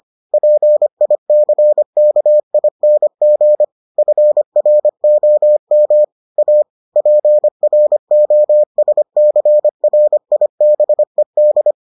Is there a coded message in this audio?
Yes: Morse code from 0.3 to 11.7 s, heard as a keyed tone, about 25 wpm, 600 Hz. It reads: PICKING FROM A PROSCRIBED